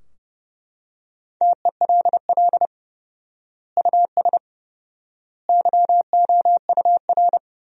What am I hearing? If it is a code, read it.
Morse code, heard as a keyed tone, about 30 wpm, 700 Hz. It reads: TELL US YOUR
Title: 4x Word-Spacing - Sentences from Top 200 Words - 30wpm